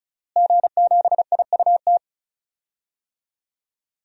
Morse code, heard as a keyed tone, about 34 wpm, 700 Hz. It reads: G7IUT
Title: DXCC-100 Call Signs - 5 Characters - Repeated 2x - 35wpm